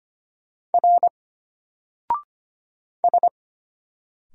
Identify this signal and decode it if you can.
Morse code, heard as a keyed tone, about 25 wpm, 700 Hz. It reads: R S
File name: Single-Letter-Number-25wpm